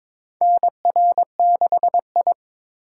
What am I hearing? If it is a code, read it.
Morse code, heard as a keyed tone, about 22 wpm, 700 Hz. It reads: NR6I
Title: USA Call Signs - 4 Characters - 4x Character Spacing - 22wpm